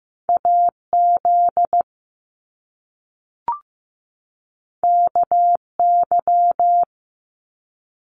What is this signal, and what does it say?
Morse code, heard as a keyed tone, about 15 wpm, 700 Hz. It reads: AZ KY
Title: US States 15wpm